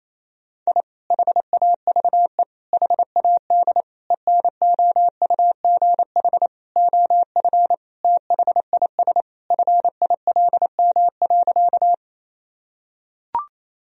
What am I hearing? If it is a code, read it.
Morse code, heard as a keyed tone, about 28 wpm, 700 Hz. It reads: I HAVE HAD ENOUGH OF THIS FILM.
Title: Sentences from Top 1000 Words - Review Part 3 - 28wpm